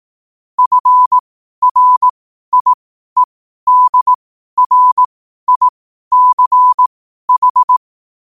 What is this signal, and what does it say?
Morse code, heard as a keyed tone, about 18 wpm, spaced as for 12 wpm, 1000 Hz. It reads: FRIEDRICH
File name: qotdfiles / 12wpm